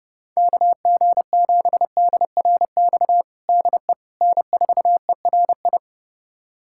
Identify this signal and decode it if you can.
Morse code, heard as a keyed tone, about 30 wpm, 700 Hz. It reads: KG7DRX DE N4ERI
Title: USA Callsign DE Callsign 30wpm